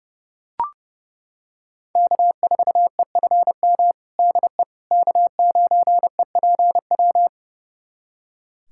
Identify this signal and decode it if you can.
Morse code, heard as a keyed tone, about 30 wpm, 700 Hz. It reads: K4EFM DE K9EPW
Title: USA Callsign DE Callsign - 5 Characters - 30wpm